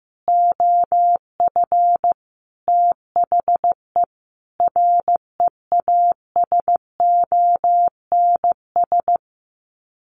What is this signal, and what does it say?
Morse code, heard as a keyed tone, about 14 wpm, 700 Hz. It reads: OF THE REASONS
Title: Sets of 5 Words Encore2 15wpm